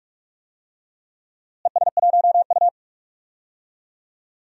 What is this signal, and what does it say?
Morse code, heard as a keyed tone, about 44 wpm, 700 Hz. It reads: ES1U
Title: DXCC-100 Call Signs - 4 Characters - Repeated 3x - 45wpm